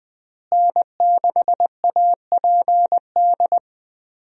Morse code, heard as a keyed tone, about 20 wpm, 700 Hz. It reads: N6APD
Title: USA Call Signs - 5 Characters - Repeated 2x - 20wpm